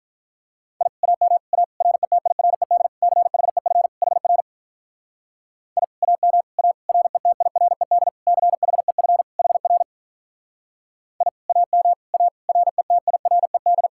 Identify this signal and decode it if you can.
Morse code, heard as a keyed tone, about 48 wpm, 700 Hz. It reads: I AM A RETIRED CHEF HR I AM A RETIRED CHEF HR I AM A RETIRED